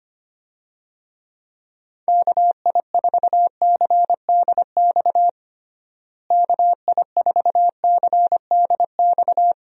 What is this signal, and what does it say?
Morse code, heard as a keyed tone, about 25 wpm, 700 Hz. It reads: KI4CDX KI4CDX
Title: USA Call Signs - 6 Characters - Repeated 2x - 25wpm